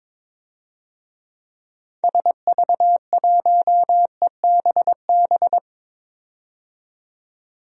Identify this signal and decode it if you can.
Morse code, heard as a keyed tone, about 22 wpm, 700 Hz. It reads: SV1EBB
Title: DXCC-100 Call Signs - 6 Characters - 22wpm